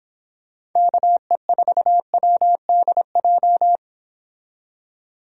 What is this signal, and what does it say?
Morse code, heard as a keyed tone, about 26 wpm, 700 Hz. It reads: KE4WDJ